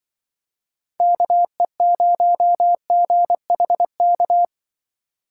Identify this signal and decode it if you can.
Morse code, heard as a keyed tone, about 24 wpm, 700 Hz. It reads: KE0GHK
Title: Speed Racing - USA Call Signs 20wpm